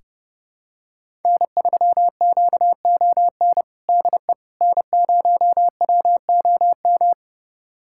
Morse code, heard as a keyed tone, about 30 wpm, 700 Hz. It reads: N3QON DE N0WOM
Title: USA Callsign DE Callsign - 5 Characters - 30wpm